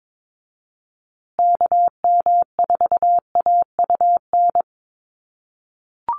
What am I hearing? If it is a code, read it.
Morse code, heard as a keyed tone, about 22 wpm, 700 Hz. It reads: KM4AUN
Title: USA Call Signs Repeated 2x 22wpm